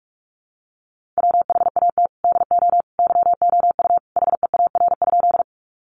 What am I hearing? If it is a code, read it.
Morse code, heard as a keyed tone, about 44 wpm, 700 Hz. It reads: WHAT DO YOU HEAR?
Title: Sentences from Top 300 Words 45wpm